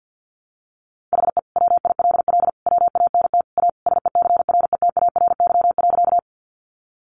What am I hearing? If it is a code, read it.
Morse code, heard as a keyed tone, about 50 wpm, 700 Hz. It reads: HE WILL WANT A SECRETARY.
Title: Sentences from Top 1300 Words 50wpm